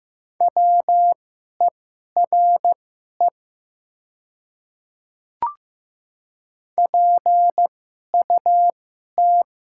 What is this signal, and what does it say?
Morse code, heard as a keyed tone, about 15 wpm, spaced as for 11 wpm, 700 Hz. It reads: WERE PUT
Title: 2x Character Spacing - Top 200 Words - 15wpm